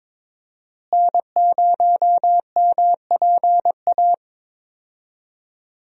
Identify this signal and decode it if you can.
Morse code, heard as a keyed tone, about 22 wpm, 700 Hz. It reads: N0MPA
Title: USA Call Signs - 5 Characters - Repeated 2x - 22wpm